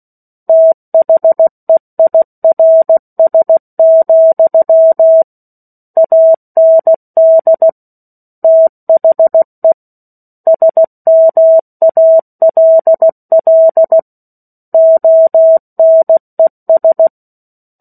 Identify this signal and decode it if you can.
Morse code, heard as a keyed tone, about 16 wpm, 650 Hz. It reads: THEIRS, AND THE SMALL ONES
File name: 03-Chapter-03 16 WPM